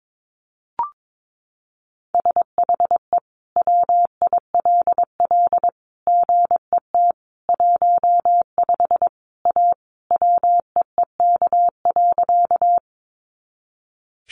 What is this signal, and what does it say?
Morse code, heard as a keyed tone, about 22 wpm, 700 Hz. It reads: SHE WILL GET 15 A WEEK.
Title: Easy Phrases 22wpm